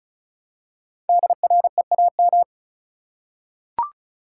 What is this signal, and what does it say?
Morse code, heard as a keyed tone, about 36 wpm, 700 Hz. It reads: DREAM
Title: Five Letter Words - 35wpm